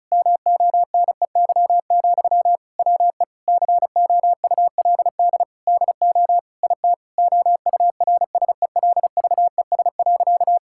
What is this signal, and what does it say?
Morse code, heard as a keyed tone, about 34 wpm, 700 Hz. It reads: MONEY, WE COULD DO IT OURSELVES.
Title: Sentences from Top 1600 Words 35wpm